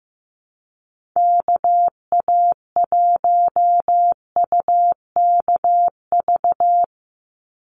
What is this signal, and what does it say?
Morse code, heard as a keyed tone, about 15 wpm, 700 Hz. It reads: KA1UKV
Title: USA Call Signs 15wpm